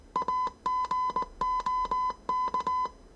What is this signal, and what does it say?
Morse code, heard as a keyed tone, about 19 wpm, 1010 Hz. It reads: AGOK
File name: 业余无线电 / ●未分类 / 字码80